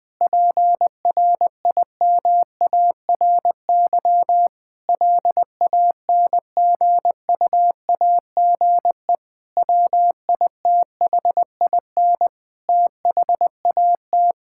Morse code, heard as a keyed tone, about 20 wpm, 700 Hz. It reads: PRIMARY LANGUAGE WITHIN THAT
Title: Sentences from Top 1000 Words - Review Part 5 - 20wpm